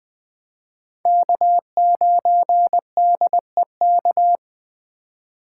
Morse code, heard as a keyed tone, about 22 wpm, 700 Hz. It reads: K9DEK